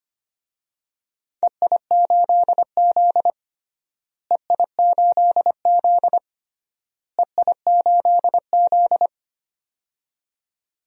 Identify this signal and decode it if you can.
Morse code, heard as a keyed tone, about 25 wpm, 700 Hz. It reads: EI8Z EI8Z EI8Z
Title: DXCC-100 Call Signs - 4 Characters - Repeated 3x - 25wpm